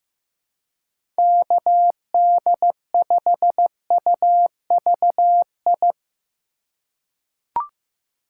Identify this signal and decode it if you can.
Morse code, heard as a keyed tone, about 15 wpm, 700 Hz. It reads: KD5UVI
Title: USA Call Signs - 6 Characters - 3x Character Spacing - 15wpm